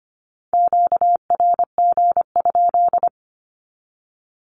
Morse code, heard as a keyed tone, about 25 wpm, 700 Hz. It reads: QRG?
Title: Q-Codes with Spoken Meaning - 25wpm